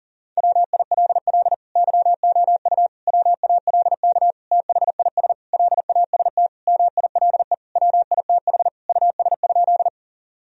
Morse code, heard as a keyed tone, about 40 wpm, 700 Hz. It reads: WILL YOU WALK THIS LAST MILE WITH US?
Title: Sentences from Top 1000 Words - Review Part 1 - 40wpm